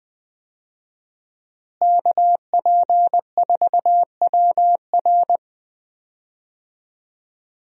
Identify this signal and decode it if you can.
Morse code, heard as a keyed tone, about 20 wpm, 700 Hz. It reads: KP4WR
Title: USA Call Signs - 5 Characters - Repeated 2x - 20wpm